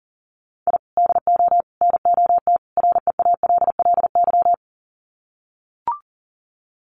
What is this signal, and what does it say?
Morse code, heard as a keyed tone, about 40 wpm, 700 Hz. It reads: I DO NOT REALLY
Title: Sets of 4 Words 40wpm